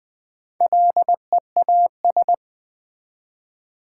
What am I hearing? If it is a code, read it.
Morse code, heard as a keyed tone, about 20 wpm, 700 Hz. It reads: LEAS
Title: Build A Word - 1 Letter At A Time - 5-Letter Words - 20wpm